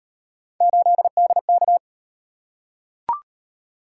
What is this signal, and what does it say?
Morse code, heard as a keyed tone, about 38 wpm, 700 Hz. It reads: 8DK